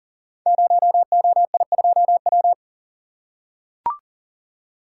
Morse code, heard as a keyed tone, about 40 wpm, 700 Hz. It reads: 0OI2W
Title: Code Groups - 5 Letter:Numbers - 40wpm